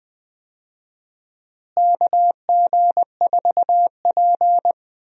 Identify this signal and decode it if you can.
Morse code, heard as a keyed tone, about 21 wpm, 700 Hz. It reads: KG4P
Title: USA Call Signs - 4 Characters - Repeated 3x - 20wpm